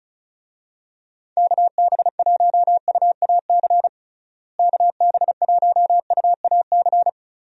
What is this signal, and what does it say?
Morse code, heard as a keyed tone, about 36 wpm, 700 Hz. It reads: KB1UAC KB1UAC
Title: USA Call Signs - 6 Characters - Repeated 3x - 35wpm